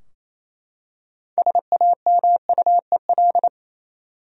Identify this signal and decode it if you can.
Morse code, heard as a keyed tone, about 27 wpm, 700 Hz. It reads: SAMUEL
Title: USA Common Names - 28wpm